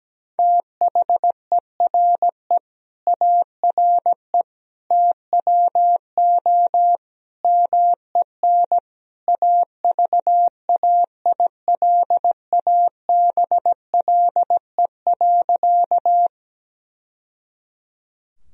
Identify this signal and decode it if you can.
Morse code, heard as a keyed tone, about 17 wpm, 700 Hz. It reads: THERE ARE TWO MEN AVAILABLE.
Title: Sentences from Top 1000 Words - Review Part 3 - 17wpm